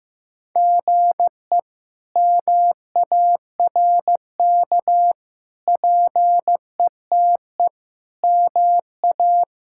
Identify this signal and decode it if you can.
Morse code, heard as a keyed tone, about 15 wpm, 700 Hz. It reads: GE MARK PETE MA
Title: K1USN SST QSO Practice - 15wpm